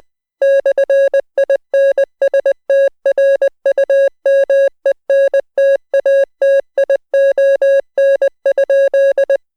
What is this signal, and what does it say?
Morse code, heard as a keyed tone, about 20 wpm, 550 Hz. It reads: /INSTRUMENTATION?